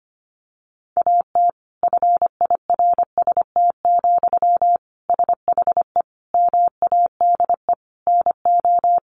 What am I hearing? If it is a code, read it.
Morse code, heard as a keyed tone, about 25 wpm, 700 Hz. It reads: AT FIRST, SHE MADE NO
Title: Sentences from Top 1400 Words 25wpm